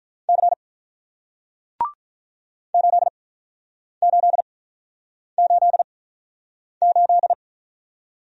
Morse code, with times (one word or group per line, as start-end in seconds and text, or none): 0.3-0.5 C
2.7-7.3 8888